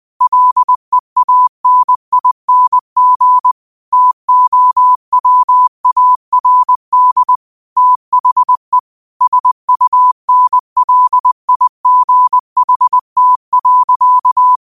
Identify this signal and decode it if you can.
Morse code, heard as a keyed tone, about 20 wpm, 1000 Hz. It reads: LEANING TOWARD THE SUNLIGHT.
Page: Daily quotes in Morse Code at 20 Words per Minute. Use this to improve your code copying skill.